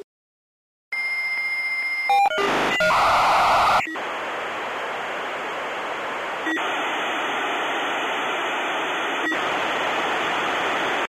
Dial-up modem handshake.
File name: Modem connection, signal, interference